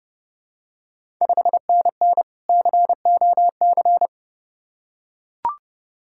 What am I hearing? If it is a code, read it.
Morse code, heard as a keyed tone, about 30 wpm, 690 Hz. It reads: 5NN COC
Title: Ontario QSO Party Exchange 30wpm